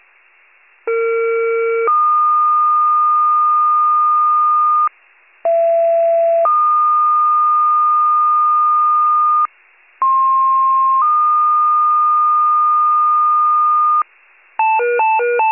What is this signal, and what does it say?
Two-tone page, heard as a one-second tone, then a three-second tone, about 460 then 1220 Hz, about 670 then 1220 Hz, about 1060 then 1220 Hz. It is heard from 0.9 to 14.0 s.
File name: wcalert